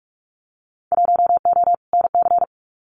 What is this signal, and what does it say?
Morse code, heard as a keyed tone, about 44 wpm, 700 Hz. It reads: 1O NC
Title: Winter Field Day Exchanges 45wpm